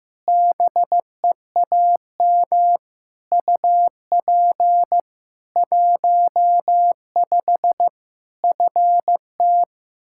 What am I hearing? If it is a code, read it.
Morse code, heard as a keyed tone, about 16 wpm, 700 Hz. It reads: BEAM UP 15 FT
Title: Practice QSO - Exchange 2 - Info - 15wpm